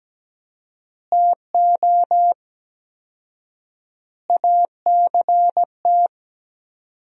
Morse code, heard as a keyed tone, about 17 wpm, 700 Hz. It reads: TO ACT
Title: 4x Word-Spacing - Sentences from Top 500 Words - 17wpm